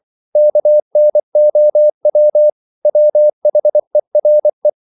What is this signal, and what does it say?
Morse code, heard as a keyed tone, about 24 wpm, 600 Hz. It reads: KNOW WHERE
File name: CW / Practice / EnglishQuotations0324